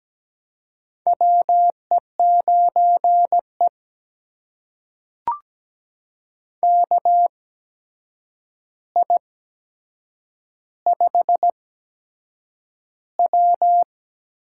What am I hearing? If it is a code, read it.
Morse code, heard as a keyed tone, about 17 wpm, spaced as for 7 wpm, 700 Hz. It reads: WE9E KI5W
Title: USA Call Signs - 4 Characters - 8x Character Spacing - 17wpm